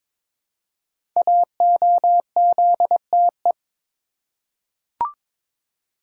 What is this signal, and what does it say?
Morse code, heard as a keyed tone, about 22 wpm, 700 Hz. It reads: AOZTE
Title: Code Groups - 5 Letters - 22wpm